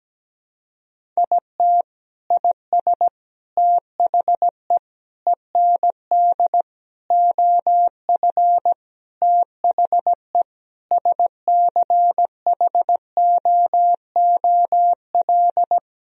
Morse code, heard as a keyed tone, about 17 wpm, 700 Hz. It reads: IT IS THE END OF THE SCHOOL
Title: Sentences from Top 1000 Words - Review Part 1 - 17wpm